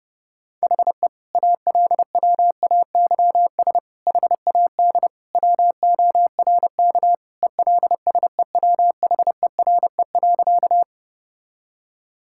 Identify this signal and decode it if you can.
Morse code, heard as a keyed tone, about 30 wpm, 700 Hz. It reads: HE ALWAYS HAD WORK ELSEWHERE.
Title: Sentences from Top 1600 Words 30wpm